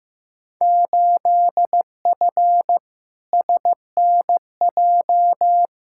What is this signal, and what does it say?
Morse code, heard as a keyed tone, about 16 wpm, 700 Hz. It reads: 8F SNJ
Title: Field Day Exchanges Encore - 15wpm